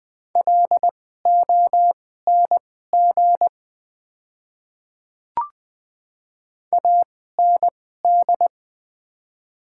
Morse code, heard as a keyed tone, about 20 wpm, spaced as for 14 wpm, 700 Hz. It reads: LONG AND
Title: 2x Character Spacing - Top 100 Words - 20wpm